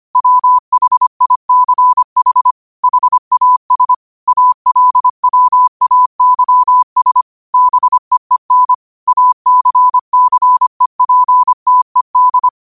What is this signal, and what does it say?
Morse code, heard as a keyed tone, about 25 wpm, 1000 Hz. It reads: WHICH HAS ALWAYS BEEN ACCEPTED
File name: qotdfiles / 25wpm